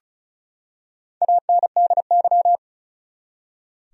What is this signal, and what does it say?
Morse code, heard as a keyed tone, about 35 wpm, 700 Hz. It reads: ANDY
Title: USA Common Names - Max 4 Letters - 35wpm